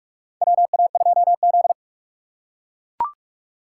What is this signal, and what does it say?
Morse code, heard as a keyed tone, about 44 wpm, 700 Hz. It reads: WA2Z
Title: USA Call Signs - 4x Character Spacing - 45wpm